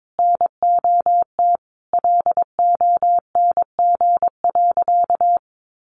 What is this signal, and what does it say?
Morse code, heard as a keyed tone, about 22 wpm, 700 Hz. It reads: NOT LONG.
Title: Sentences from Top 1300 Words 22wpm